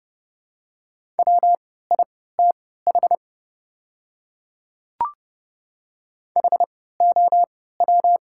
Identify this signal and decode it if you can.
Morse code, heard as a keyed tone, about 32 wpm, spaced as for 17 wpm, 700 Hz. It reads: WITH HOW
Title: 3x Character Spacing - Top 100 Words - 30wpm